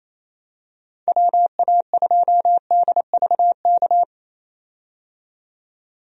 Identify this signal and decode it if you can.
Morse code, heard as a keyed tone, about 28 wpm, 700 Hz. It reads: WA2DVK
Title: USA Call Signs - 6 Characters - 28wpm